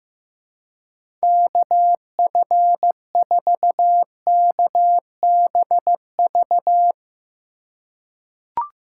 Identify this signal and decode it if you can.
Morse code, heard as a keyed tone, about 15 wpm, 700 Hz. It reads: KF4KBV
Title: USA Call Signs - 6 Characters - 3x Character Spacing - 15wpm